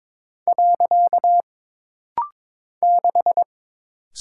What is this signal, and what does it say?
Morse code, heard as a keyed tone, about 22 wpm, 700 Hz. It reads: . 6
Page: This rapid-fire timing is geared to push your ICR (Instant Character Recognition) skills.